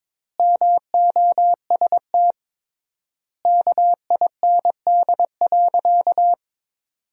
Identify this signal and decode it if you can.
Morse code, heard as a keyed tone, about 22 wpm, 700 Hz. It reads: MOST KIND.
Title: 3x Word-Spacing - Sentences from Top 200 Words - 22wpm